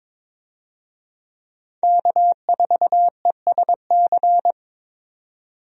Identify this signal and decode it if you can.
Morse code, heard as a keyed tone, about 22 wpm, 700 Hz. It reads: K4ESC
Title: USA Call Signs - 5 Characters - 22wpm